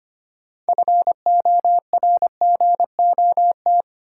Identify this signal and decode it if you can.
Morse code, heard as a keyed tone, about 25 wpm, 700 Hz. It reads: FORGOT
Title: Sentences from Top 1400 Words - New Word Drill - 25wpm